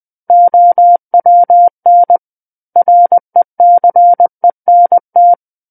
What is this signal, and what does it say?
Morse code, heard as a keyed tone, about 20 wpm, 700 Hz. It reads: OWN RECENT
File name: 20 WPM